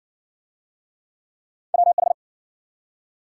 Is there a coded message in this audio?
Yes: Morse code from 1.7 to 2.1 s, heard as a keyed tone, about 54 wpm, 700 Hz. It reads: WH